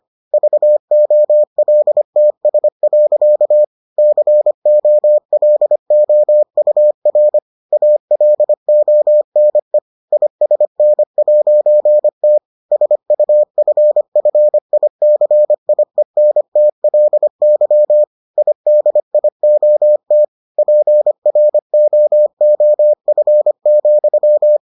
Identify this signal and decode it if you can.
Morse code, heard as a keyed tone, about 25 wpm, 600 Hz. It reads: VOLTS. COLOUR ALONE ISN'T SUFFICIENTLY IDIOT PROOF,